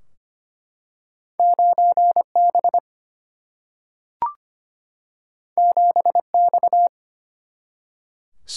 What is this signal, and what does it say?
Morse code, heard as a keyed tone, about 25 wpm, 700 Hz. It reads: 9 B 7 X
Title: Single-Letter Single-Number 25wpm